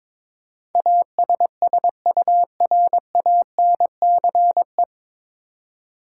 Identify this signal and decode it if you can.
Morse code, heard as a keyed tone, about 22 wpm, 700 Hz. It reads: ASSURANCE
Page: Morse Code Ninja practice 22wpm